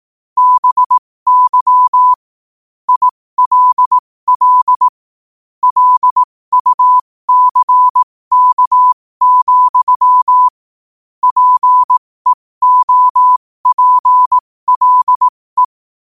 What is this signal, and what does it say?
Morse code, heard as a keyed tone, about 18 wpm, spaced as for 15 wpm, 1000 Hz. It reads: BY ILL LUCK, PEOPLE
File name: qotdfiles / 15wpm